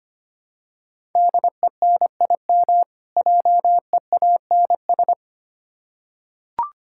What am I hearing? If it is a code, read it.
Morse code, heard as a keyed tone, about 25 wpm, 700 Hz. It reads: DENIM JEANS
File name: practice / 25wpm / Difficult Words with Hints - 25wpm